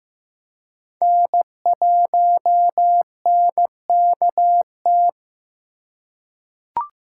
Morse code, heard as a keyed tone, about 15 wpm, 700 Hz. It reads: N1NKT
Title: USA Call Signs - 5 Characters - 3x Character Spacing - 15wpm